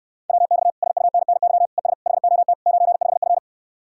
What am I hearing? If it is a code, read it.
Morse code, heard as a keyed tone, about 69 wpm, 700 Hz. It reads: WX SUNNY ES HOT 95F